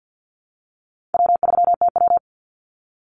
Morse code, heard as a keyed tone, about 50 wpm, 700 Hz. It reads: W3TW
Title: Call Signs - 50wpm